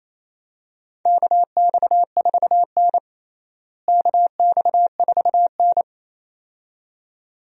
Morse code, heard as a keyed tone, about 28 wpm, 700 Hz. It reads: KX4N KX4N
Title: USA Call Signs - 4 Characters - Repeated 2x - 28wpm